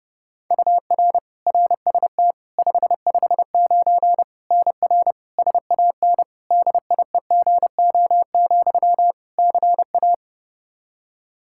Morse code, heard as a keyed tone, about 30 wpm, 700 Hz. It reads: UR RST 559 NR SAN DIEGO, CA
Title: Practice QSO - Exchange 1 - Beginning - 30wpm